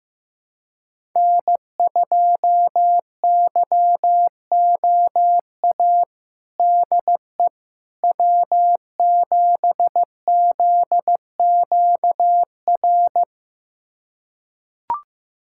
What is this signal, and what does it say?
Morse code, heard as a keyed tone, about 15 wpm, 700 Hz. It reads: N2YOA DE W7ZQR
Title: USA Callsign DE Callsign - 5 Characters - 15wpm